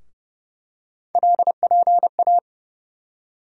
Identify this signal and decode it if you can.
Morse code, heard as a keyed tone, about 30 wpm, 700 Hz. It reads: LPA
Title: USA Call Sign Postfixes - 3 Letters 30wpm